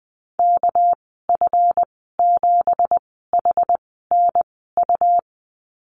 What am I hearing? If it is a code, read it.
Morse code, heard as a keyed tone, about 20 wpm, spaced as for 15 wpm, 700 Hz. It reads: KF7HNU
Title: 219 Lesson 35g - Number 0 - Call Signs 20wpm